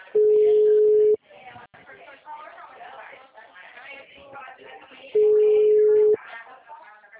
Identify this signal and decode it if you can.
Ringback tone (Europe, 425 Hz), with speech.